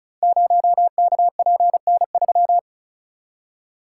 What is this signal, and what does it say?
Morse code, heard as a keyed tone, about 36 wpm, 700 Hz. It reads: 0KPN3
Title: Code Groups - 5 Letter:Numbers - 35wpm